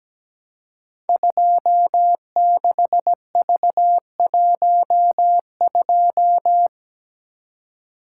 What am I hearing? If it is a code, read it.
Morse code, heard as a keyed tone, about 17 wpm, 700 Hz. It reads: 26V12
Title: Code Groups - 5 Letter:Numbers - 17wpm